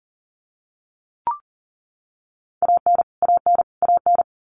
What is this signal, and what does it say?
Morse code, heard as a keyed tone, about 40 wpm, 700 Hz. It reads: AN AN AN
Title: Top 500 Words Repeated 40wpm